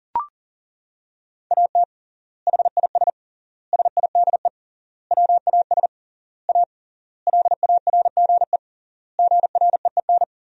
Morse code, heard as a keyed tone, about 40 wpm, 700 Hz. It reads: AT HIS SIDE WAS A LARGE GREEN
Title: 3x Word-Spacing - Sentences from Top 500 Words - 40wpm